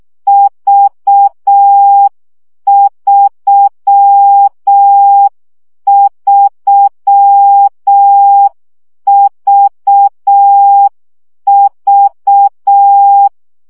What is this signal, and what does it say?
Morse code, heard as a keyed tone, about 6 wpm, 800 Hz. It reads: V33VV
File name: CW gyakorlas v 3 5 baud